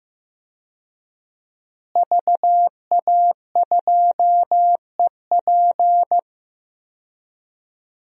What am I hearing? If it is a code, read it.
Morse code, heard as a keyed tone, about 16 wpm, 700 Hz. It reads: VA2EP
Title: DXCC-100 Call Signs - 5 Characters - Repeated 2x - 15wpm